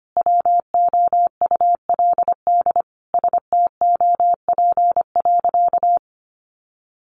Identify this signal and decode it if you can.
Morse code, heard as a keyed tone, about 25 wpm, 700 Hz. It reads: WOULD STOP.
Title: Sentences from Top 1300 Words 25wpm